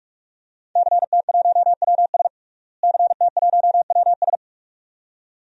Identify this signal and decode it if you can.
Morse code, heard as a keyed tone, about 46 wpm, 700 Hz. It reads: CT1WS CT1WS
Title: DXCC-100 Call Signs - 5 Characters - Repeated 2x - 45wpm